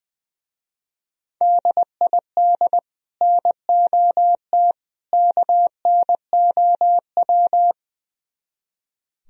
Morse code, heard as a keyed tone, about 20 wpm, 700 Hz. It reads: DID NOT KNOW
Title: Speed-Racing - Sets of 3 Words 20wpm